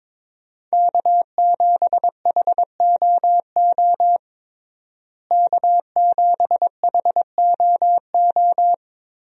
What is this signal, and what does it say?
Morse code, heard as a keyed tone, about 21 wpm, 700 Hz. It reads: K7HOO K7HOO
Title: USA Call Signs - 5 Characters - Repeated 3x - 22wpm